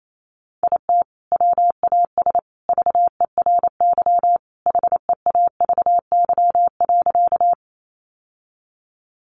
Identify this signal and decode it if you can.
Morse code, heard as a keyed tone, about 28 wpm, 700 Hz. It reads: IT WAS VERY HEAVY.
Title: Sentences from Top 1100 Words 28wpm